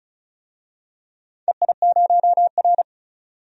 Morse code, heard as a keyed tone, about 36 wpm, 700 Hz. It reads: EI0R
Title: DXCC-100 Call Signs - 4 Characters - 35wpm